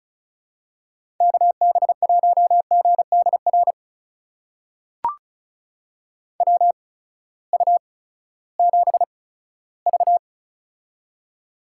Morse code, heard as a keyed tone, about 36 wpm, spaced as for 18 wpm, 700 Hz. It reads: KB1GDR WU7V
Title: USA Call Signs - 8x Character Spacing - 35wpm